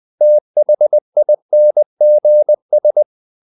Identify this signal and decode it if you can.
Morse code, heard as a keyed tone, about 20 wpm, 600 Hz. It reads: THINGS
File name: CW / Practice / If20